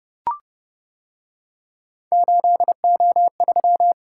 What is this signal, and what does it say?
Morse code, heard as a keyed tone, about 30 wpm, 700 Hz. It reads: 8O3